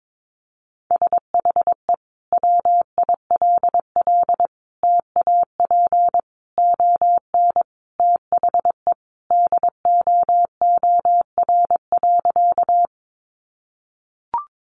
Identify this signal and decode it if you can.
Morse code, heard as a keyed tone, about 22 wpm, 700 Hz. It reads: SHE WILL TAP ON THE DOOR.
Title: Sentences from Top 1400 Words 22wpm